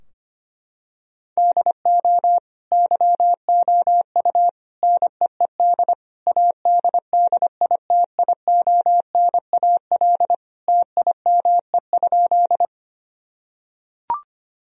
Morse code, heard as a keyed tone, about 25 wpm, 700 Hz. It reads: DO YOU NEED ADDITIONAL TIME?
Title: Sentences from Top 1000 Words - Review Part 4 - 25wpm